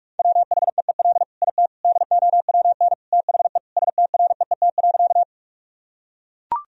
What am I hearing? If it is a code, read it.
Morse code, heard as a keyed tone, about 41 wpm, 700 Hz. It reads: WHEEL IT DOWN THE STREET.
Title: Sentences from Top 1000 Words - Review Part 3 - 45wpm